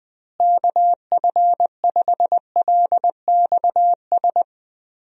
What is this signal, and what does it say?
Morse code, heard as a keyed tone, about 20 wpm, 700 Hz. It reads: KF5LXS
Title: USA Call Signs - 6 Characters - Repeated 2x - 20wpm